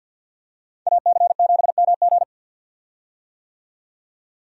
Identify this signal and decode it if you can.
Morse code, heard as a keyed tone, about 52 wpm, 700 Hz. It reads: AC7MG
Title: USA Call Signs - 5 Characters - 50wpm